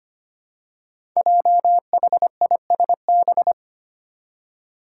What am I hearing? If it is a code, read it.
Morse code, heard as a keyed tone, about 25 wpm, 700 Hz. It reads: JHISB
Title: Five Challenging Letter-Number Combinations 25wpm